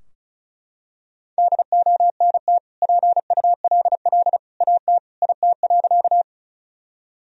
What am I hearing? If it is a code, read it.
Morse code, heard as a keyed tone, about 36 wpm, 700 Hz. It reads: DONT PULL AT IT.
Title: Sentences from Top 1000 Words - Review Part 2 - 35wpm